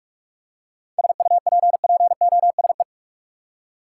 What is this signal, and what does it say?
Morse code, heard as a keyed tone, about 43 wpm, 700 Hz. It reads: SUPPOSE
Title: Seven Letter Words - 45wpm